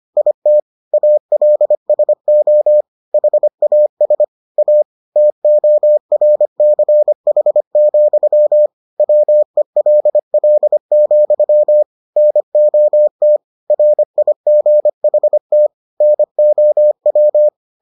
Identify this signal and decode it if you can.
Morse code, heard as a keyed tone, about 25 wpm, 600 Hz. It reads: IT ALSO HAS A TORCH, WELL, NOT RIGHT NOW